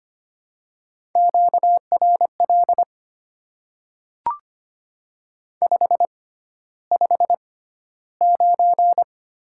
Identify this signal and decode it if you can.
Morse code, heard as a keyed tone, about 25 wpm, spaced as for 13 wpm, 700 Hz. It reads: QRL 559
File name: Top 90 QSO Elements - 6x Character Spacing - 25wpm